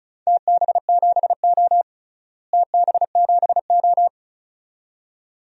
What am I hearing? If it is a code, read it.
Morse code, heard as a keyed tone, about 34 wpm, 700 Hz. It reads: TB7O TB7O
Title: DXCC-100 Call Signs - 4 Characters - Repeated 2x - 35wpm